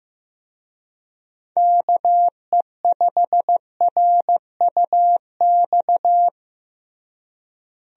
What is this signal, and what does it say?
Morse code, heard as a keyed tone, about 15 wpm, 700 Hz. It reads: KE5RUX
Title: USA Call Signs - 6 Characters - Repeated 2x - 15wpm